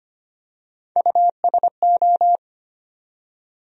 Morse code, heard as a keyed tone, about 25 wpm, 700 Hz. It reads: USO
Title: USA Call Sign Postfixes - 3 Letters 25wpm